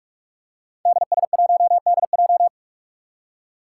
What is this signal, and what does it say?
Morse code, heard as a keyed tone, about 44 wpm, 700 Hz. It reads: DS1DJ
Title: Five Challenging Letter-Number Combinations 45wpm